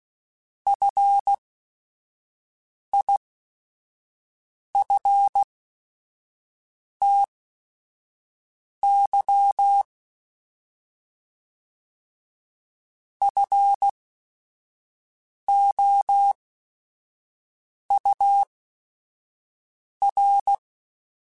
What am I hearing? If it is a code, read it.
Morse code, heard as a keyed tone, about 16 wpm, spaced as for 5 wpm, 800 Hz. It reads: FIFTY FOUR
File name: F Runs / F Runs mp3s / F Runs @5